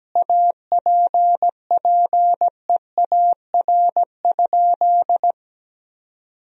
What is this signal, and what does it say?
Morse code, heard as a keyed tone, about 17 wpm, 700 Hz. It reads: APPEAR?
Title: Sentences from Top 1000 Words - Review Part 2 - 17wpm